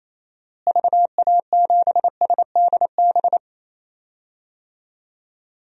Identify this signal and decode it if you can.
Morse code, heard as a keyed tone, about 28 wpm, 700 Hz. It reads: VA7SDB
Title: DXCC-100 Call Signs - 6 Characters - Repeated 2x - 28wpm